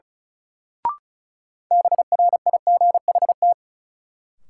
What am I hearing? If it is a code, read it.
Morse code, heard as a keyed tone, about 36 wpm, 700 Hz. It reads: BRIGHT
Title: Sentences from Top 1000 Words - New Word Drill - 35wpm